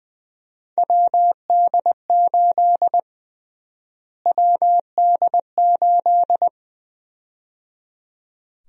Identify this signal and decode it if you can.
Morse code, heard as a keyed tone, about 20 wpm, 700 Hz. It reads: WD8 WD8
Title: USA Call Sign Prefixes - Repeated 2x - 20wpm